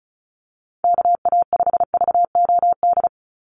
Morse code, heard as a keyed tone, about 34 wpm, 700 Hz. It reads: KA5VOD
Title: USA Call Signs 35wpm